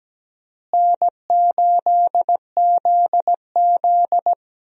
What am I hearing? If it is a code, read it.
Morse code, heard as a keyed tone, about 17 wpm, 700 Hz. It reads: N8ZZ
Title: USA Call Signs - 4 Characters - Repeated 2x - 17wpm